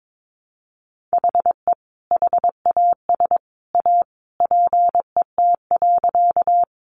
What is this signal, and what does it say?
Morse code, heard as a keyed tone, about 22 wpm, 700 Hz. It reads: HE HAS A PET.
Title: Easy Phrases 22wpm